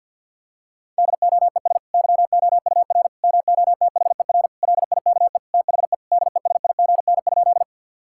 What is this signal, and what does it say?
Morse code, heard as a keyed tone, about 47 wpm, 700 Hz. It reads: DOES YOUR MOTHER LIKE THE DESIGN?
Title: Sentences from Top 1000 Words - Review Part 5 - 50wpm